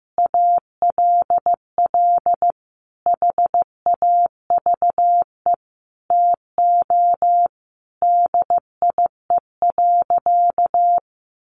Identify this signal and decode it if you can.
Morse code, heard as a keyed tone, about 14 wpm, 700 Hz. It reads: ALL HAVE TO DIE.
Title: Sentences from Top 1300 Words 15wpm